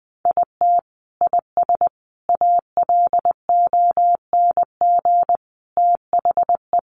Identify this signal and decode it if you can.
Morse code, heard as a keyed tone, about 20 wpm, 700 Hz. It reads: IT IS ALONG THE
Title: Sentences from Top 1200 Words 20wpm